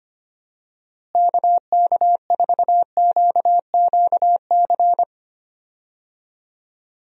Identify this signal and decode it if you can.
Morse code, heard as a keyed tone, about 25 wpm, 700 Hz. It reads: KK4QQC
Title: USA Call Signs - 6 Characters - Repeated 3x - 25wpm